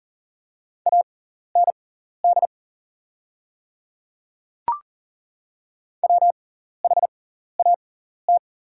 Morse code, heard as a keyed tone, about 41 wpm, spaced as for 14 wpm, 690 Hz. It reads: AND WHAT
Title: 6x Character Spacing - Top 100 Words - 40wpm